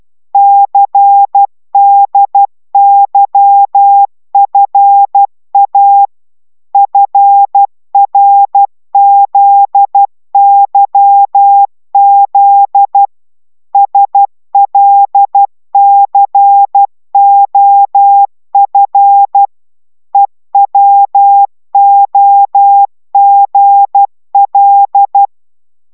Morse code, heard as a keyed tone, about 12 wpm, 800 Hz. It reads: CDYFA FRZYZ SLCOF EWOGL